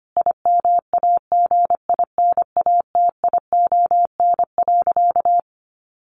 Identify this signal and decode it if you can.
Morse code, heard as a keyed tone, about 25 wpm, 700 Hz. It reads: IMAGINATION.
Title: Sentences from Top 1300 Words 25wpm